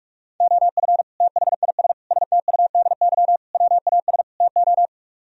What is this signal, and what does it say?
Morse code, heard as a keyed tone, about 44 wpm, 700 Hz. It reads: OF THIS STUDY WAS TO